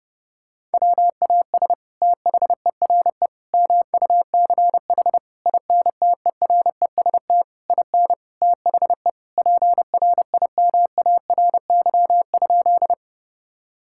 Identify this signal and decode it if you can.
Morse code, heard as a keyed tone, about 30 wpm, 700 Hz. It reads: WAS THERE MUCH INTEREST IN THE PRIMARY?
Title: Sentences from Top 1000 Words - Review Part 5 - 30wpm